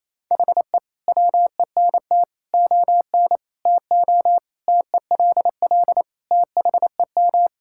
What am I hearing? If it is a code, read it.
Morse code, heard as a keyed tone, about 28 wpm, 700 Hz. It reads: HE WENT ON TO TELL THEM
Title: Sentences from Top 1000 Words - Review Part 6 - 28wpm